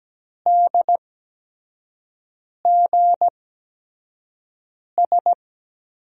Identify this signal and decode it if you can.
Morse code, heard as a keyed tone, about 17 wpm, spaced as for 5 wpm, 700 Hz. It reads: DGS
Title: USA Call Signs - 8x Character Spacing - 17wpm